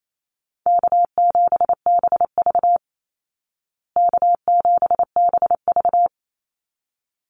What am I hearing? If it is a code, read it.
Morse code, heard as a keyed tone, about 28 wpm, 700 Hz. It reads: K7BV K7BV
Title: Call Signs Repeated 3x - 28wpm